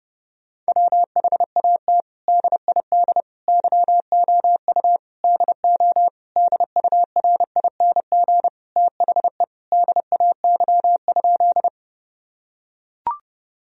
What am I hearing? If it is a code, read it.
Morse code, heard as a keyed tone, about 30 wpm, 700 Hz. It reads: WHAT DID YOU DO DURING THE DAY?
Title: Sentences from Top 1000 Words - Review Part 2 - 30wpm